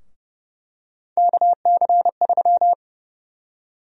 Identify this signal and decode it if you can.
Morse code, heard as a keyed tone, about 30 wpm, 700 Hz. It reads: KC3
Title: USA Call Sign Prefixes - Repeated 2x - 30wpm